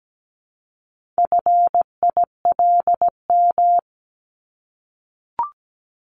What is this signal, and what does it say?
Morse code, heard as a keyed tone, about 17 wpm, 700 Hz. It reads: FILM
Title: Top 501-750 Words 17wpm